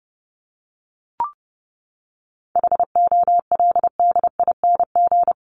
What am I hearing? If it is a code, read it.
Morse code, heard as a keyed tone, about 28 wpm, 700 Hz. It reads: HOLDING
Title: ING Suffix - Top 100 Words with Base Words 30wpm